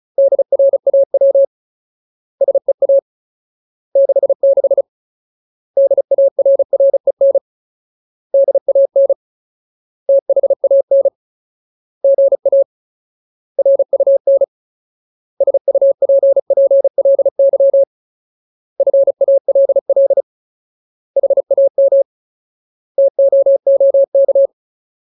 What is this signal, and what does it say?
Morse code, heard as a keyed tone, about 36 wpm, 550 Hz. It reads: DRAW SEA 66 DARREN DAN THAN GA RUN SUPPLY FALL HAM TOOK
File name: Ultimate Word List / 35WPM Ultimate